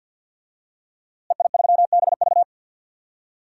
Morse code, heard as a keyed tone, about 53 wpm, 700 Hz. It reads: EI3BV
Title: DXCC-100 Call Signs - 5 Characters - Repeated 3x - 50wpm